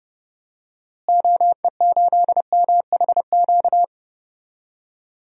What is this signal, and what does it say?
Morse code, heard as a keyed tone, about 30 wpm, 700 Hz. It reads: OE8MHQ
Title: DXCC-100 Call Signs - 6 Characters - Repeated 2x - 30wpm